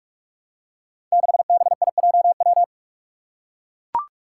Morse code, heard as a keyed tone, about 44 wpm, 700 Hz. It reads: 6BIJW
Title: Five Challenging Letter-Number Combinations 45wpm